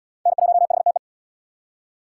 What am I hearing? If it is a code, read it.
Morse code, heard as a keyed tone, about 68 wpm, 700 Hz. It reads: N1HTE